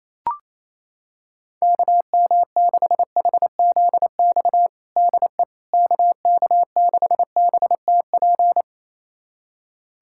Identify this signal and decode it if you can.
Morse code, heard as a keyed tone, about 28 wpm, 700 Hz. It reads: KM6HZX DE KK6BTP
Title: USA Callsign DE Callsign 28wpm